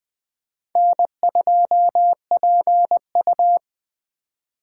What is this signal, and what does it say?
Morse code, heard as a keyed tone, about 19 wpm, 700 Hz. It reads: N2PU